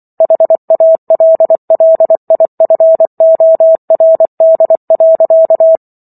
Morse code, heard as a keyed tone, about 24 wpm, 650 Hz. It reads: HALLIFORD.